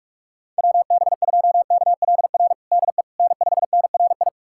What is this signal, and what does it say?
Morse code, heard as a keyed tone, about 46 wpm, 700 Hz. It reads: WB2KLR DE N5NRI
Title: USA Callsign DE Callsign 45wpm